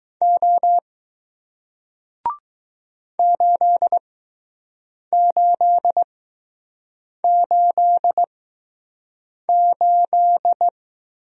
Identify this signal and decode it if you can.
Morse code, heard as a keyed tone, about 20 wpm, spaced as for 6 wpm, 700 Hz. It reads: O 8888